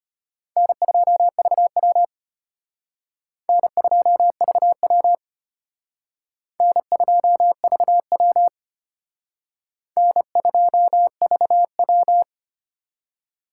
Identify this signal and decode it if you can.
Morse code, heard as a keyed tone, about 34 wpm, 700 Hz. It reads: N2VW N2VW N2VW N2VW